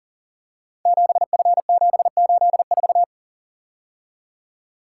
Morse code, heard as a keyed tone, about 38 wpm, 700 Hz. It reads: 7F784
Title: Code Groups - 5 Letter:Numbers - 40wpm